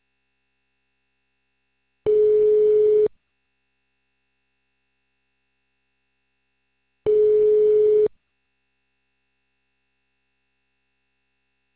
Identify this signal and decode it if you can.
Ringback tone (Europe, 425 Hz).